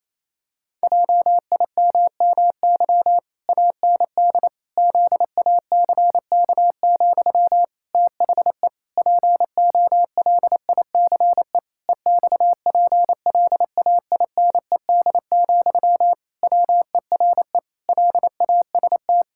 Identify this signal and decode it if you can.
Morse code, heard as a keyed tone, about 28 wpm, 700 Hz. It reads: JIMMY AND ZACK, THE POLICE EXPLAINED, WERE LAST
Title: Pangram Challenge v2 - 28wpm